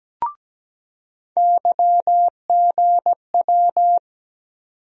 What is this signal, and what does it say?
Morse code, heard as a keyed tone, about 17 wpm, 700 Hz. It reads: YGW
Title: USA Call Sign Postfixes - 3 Letters 17wpm